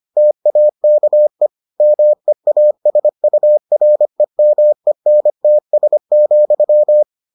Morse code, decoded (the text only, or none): TAKE MEASUREMENTS,